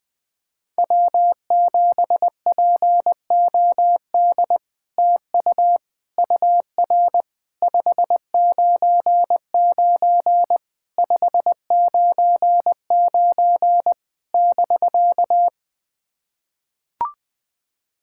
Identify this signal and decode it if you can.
Morse code, heard as a keyed tone, about 20 wpm, 700 Hz. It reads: W7POD TU UR 599 599 <BK>
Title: SOTA QSO Practice 20wpm